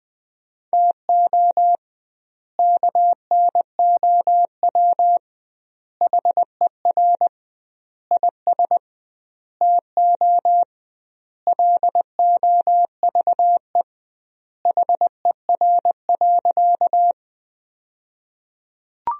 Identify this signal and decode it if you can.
Morse code, heard as a keyed tone, about 20 wpm, 700 Hz. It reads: TO KNOW HER IS TO LOVE HER.
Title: 2x Word-Spacing - Sentences from Top 400 Words - 20wpm